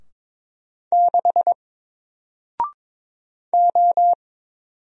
Morse code, heard as a keyed tone, about 22 wpm, 700 Hz. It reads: 6 O
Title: Single Letter-Number Alt 22wpm